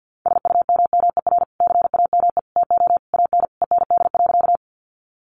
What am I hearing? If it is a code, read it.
Morse code, heard as a keyed tone, about 50 wpm, 700 Hz. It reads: SUMMER CAME TO AN END.
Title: Sentences from Top 1100 Words 50wpm